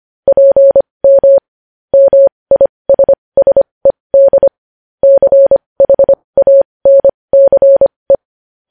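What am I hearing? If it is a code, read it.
Morse code, heard as a keyed tone, about 25 wpm, spaced as for 20 wpm, 550 Hz. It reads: PM MISSED CHANCE
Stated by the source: Morse practice of news headlines at 20 wpm.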